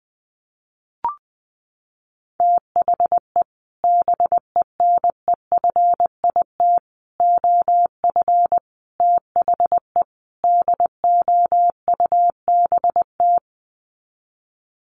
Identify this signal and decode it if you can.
Morse code, heard as a keyed tone, about 20 wpm, 700 Hz. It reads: THE BENEFIT OF THE DOUBT
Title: Sets of 5 Words Encore 20wpm